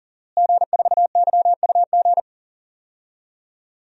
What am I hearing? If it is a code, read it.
Morse code, heard as a keyed tone, about 37 wpm, 700 Hz. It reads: G4YUG
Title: DXCC-100 Call Signs - 5 Characters - Repeated 3x - 40wpm